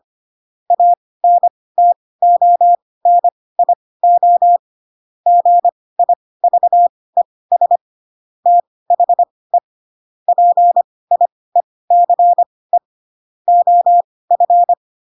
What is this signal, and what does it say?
Morse code, heard as a keyed tone, about 25 wpm, spaced as for 18 wpm, 700 Hz. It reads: ANTONIO GIVES THE PIECE OF
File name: Pinocchio / Pinocchio ch2 12 25 ik1hge